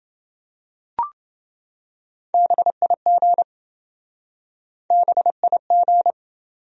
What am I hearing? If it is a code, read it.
Morse code, heard as a keyed tone, about 27 wpm, 700 Hz. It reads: BIG BIG